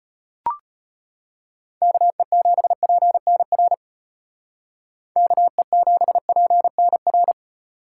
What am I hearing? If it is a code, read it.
Morse code, heard as a keyed tone, about 33 wpm, 700 Hz. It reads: KE7PNR KE7PNR